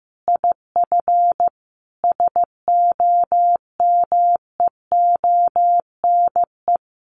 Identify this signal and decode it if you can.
Morse code, heard as a keyed tone, about 15 wpm, 700 Hz. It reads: IF SOMEONE
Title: A Sound Of Thunder 15wpm